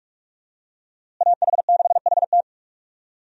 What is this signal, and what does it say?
Morse code, heard as a keyed tone, about 43 wpm, 700 Hz. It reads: AH6HT
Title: USA Call Signs - 5 Characters - Repeated 3x - 45wpm